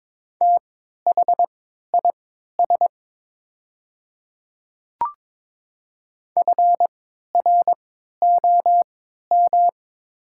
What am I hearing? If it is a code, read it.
Morse code, heard as a keyed tone, about 22 wpm, spaced as for 12 wpm, 700 Hz. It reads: THIS FROM
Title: 3x Character Spacing - Top 100 Words - 22wpm